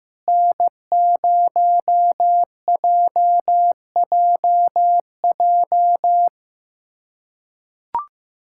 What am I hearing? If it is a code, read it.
Morse code, heard as a keyed tone, about 15 wpm, 700 Hz. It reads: N0JJJ
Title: USA Callsign DE Callsign - 5 Characters - 15wpm